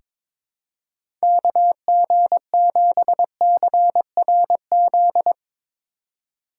Morse code, heard as a keyed tone, about 22 wpm, 700 Hz. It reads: KG7CRZ
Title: USA Call Signs - 4x Character Spacing - 22wpm